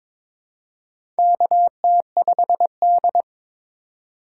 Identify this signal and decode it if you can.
Morse code, heard as a keyed tone, about 22 wpm, 700 Hz. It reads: KT5D
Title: USA Call Signs - 4 Characters - 4x Character Spacing - 22wpm